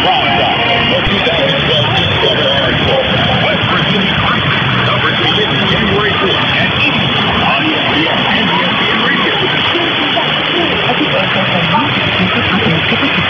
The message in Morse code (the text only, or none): none